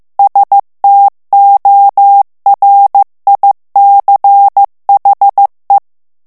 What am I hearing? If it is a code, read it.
Morse code, heard as a keyed tone, about 15 wpm, 800 Hz. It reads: STORICHE
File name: morse mp3 / ITALIEN texte clair 1 vitesse 14 mots minute